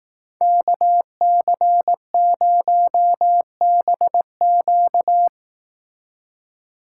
Morse code, heard as a keyed tone, about 18 wpm, 700 Hz. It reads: KC0BQ
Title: Speed Racing - USA 2x2 Call Signs 15wpm